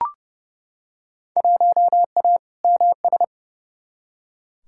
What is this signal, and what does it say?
Morse code, heard as a keyed tone, about 30 wpm, 700 Hz. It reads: 1A MS
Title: Field Day Exchanges - 30wpm